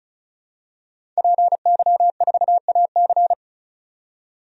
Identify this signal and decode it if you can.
Morse code, heard as a keyed tone, about 36 wpm, 700 Hz. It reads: PY4AC
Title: DXCC-100 Call Signs - 5 Characters - 35wpm